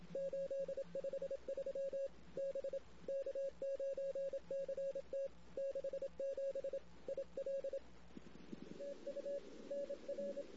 Morse code, heard as a keyed tone, about 27 wpm, 550 Hz. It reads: 853 B K9CT 67 IL TU NR
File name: CW / 2008-ss-cw-k5tr-op-n5rz -16:00